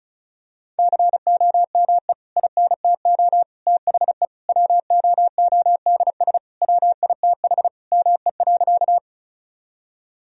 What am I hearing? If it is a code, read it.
Morse code, heard as a keyed tone, about 34 wpm, 700 Hz. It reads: COME INTO THE WOODS WITH ME.
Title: Sentences from Top 1000 Words - Review Part 2 - 35wpm